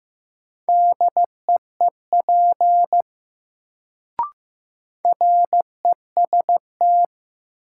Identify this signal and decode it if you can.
Morse code, heard as a keyed tone, about 15 wpm, 700 Hz. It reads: DEEP REST
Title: Sentences from Top 500 Words - New Word Drill - 15wpm